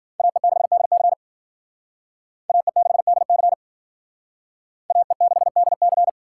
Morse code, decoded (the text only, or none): AE6DC AE6DC AE6DC